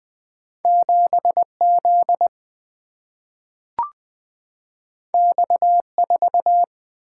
Morse code, heard as a keyed tone, about 20 wpm, 700 Hz. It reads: 7 Z X 4
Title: Single-Letter Single-Number 20wpm